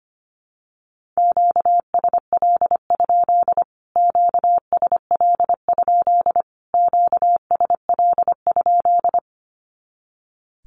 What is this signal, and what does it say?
Morse code, heard as a keyed tone, about 25 wpm, 700 Hz. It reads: QSL? QSL? QSL?
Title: Q-Codes Repeated 3x 25wpm